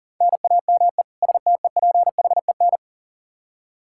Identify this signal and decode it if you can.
Morse code, heard as a keyed tone, about 40 wpm, 700 Hz. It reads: NAME STEPHEN
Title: Practice QSO - Exchange 1 - Beginning - 40wpm